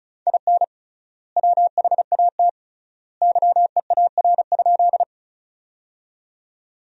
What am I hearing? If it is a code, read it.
Morse code, heard as a keyed tone, about 36 wpm, 700 Hz. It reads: IN WHAT YEAR?
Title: 3x Word-Spacing - Sentences from Top 200 Words - 35wpm